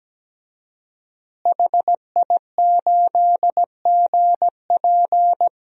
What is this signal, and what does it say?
Morse code, heard as a keyed tone, about 17 wpm, 700 Hz. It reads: HI8GP
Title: DXCC-100 Call Signs - 5 Characters - Repeated 2x - 17wpm